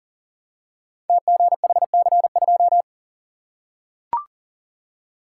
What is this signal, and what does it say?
Morse code, heard as a keyed tone, about 41 wpm, 690 Hz. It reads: TGHC2
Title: Code Groups - 5 Letter:Numbers - 40wpm